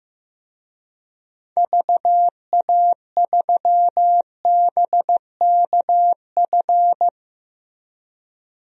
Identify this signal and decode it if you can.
Morse code, heard as a keyed tone, about 15 wpm, 700 Hz. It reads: VA3BKF
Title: DXCC-100 Call Signs - 6 Characters - Repeated 3x - 15wpm